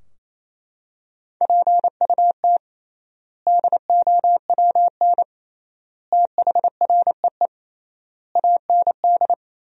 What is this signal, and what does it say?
Morse code, heard as a keyed tone, about 28 wpm, 700 Hz. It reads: PUT DOWN THREE AND
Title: 3x Word-Spacing - Sentences from Top 400 Words - 28wpm